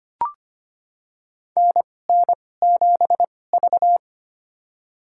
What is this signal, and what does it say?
Morse code, heard as a keyed tone, about 25 wpm, spaced as for 18 wpm, 700 Hz. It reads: NN7V
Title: USA Call Signs - 4 Characters - 2x Character Spacing - 25wpm